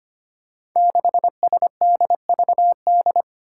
Morse code, heard as a keyed tone, about 25 wpm, 700 Hz. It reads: 6SDVD
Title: Five Challenging Letter-Number Combinations 25wpm